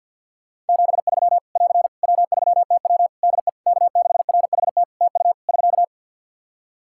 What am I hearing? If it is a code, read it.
Morse code, heard as a keyed tone, about 50 wpm, 700 Hz. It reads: 73 <AR> W3TW DE K6RHT TU <SK>
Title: Practice QSO - Exchange 4 - Ending - 50wpm